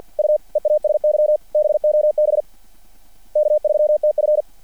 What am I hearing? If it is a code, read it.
Morse code, heard as a keyed tone, about 50 wpm, 600 Hz. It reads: U EASY BOB K3TF
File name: archived / CRSnet